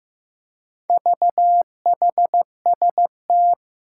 Morse code, heard as a keyed tone, about 15 wpm, 700 Hz. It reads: VHST
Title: Four Challenging Letter-Number Combinations 15wpm